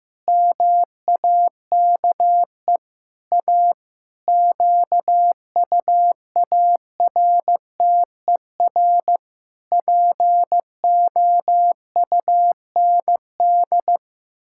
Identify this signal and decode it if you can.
Morse code, heard as a keyed tone, about 15 wpm, 700 Hz. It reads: MAKE A QUARTER POUND
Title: Pangram Challenge v2 - 15wpm